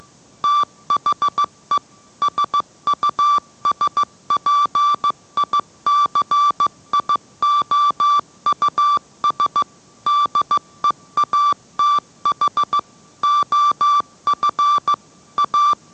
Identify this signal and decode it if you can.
Morse code, heard as a keyed tone, about 15 wpm, 1170 Hz. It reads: THE SUSPICIOUS DEATH OF A